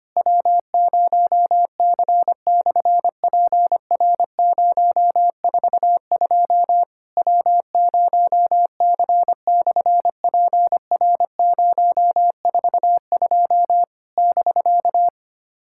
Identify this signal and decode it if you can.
Morse code, heard as a keyed tone, about 25 wpm, 700 Hz. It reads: W0C/PR042 W0C/PR042 <BK>
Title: SOTA QSO Practice 25wpm